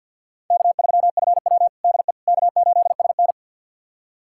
Morse code, heard as a keyed tone, about 51 wpm, 700 Hz. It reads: K3FW DE K8SN
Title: USA Callsign DE Callsign - 4 Characters - 50wpm